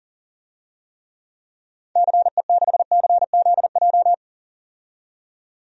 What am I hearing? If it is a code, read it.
Morse code, heard as a keyed tone, about 40 wpm, 700 Hz. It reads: CE6CZJ
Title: DXCC-100 Call Signs - 6 Characters - 40wpm